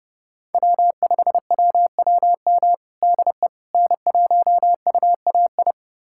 Morse code, heard as a keyed tone, about 28 wpm, 700 Hz. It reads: W5WWM DE N1UAI
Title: USA Callsign DE Callsign - 5 Characters - 30wpm